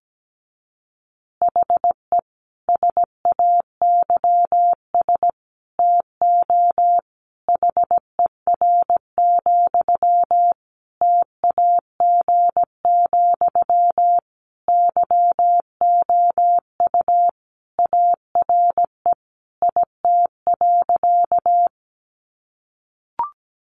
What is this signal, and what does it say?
Morse code, heard as a keyed tone, about 17 wpm, 700 Hz. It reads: HE SAYS TO HER, TAG, YOU ARE IT.
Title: Additional Easy Phrases 17wpm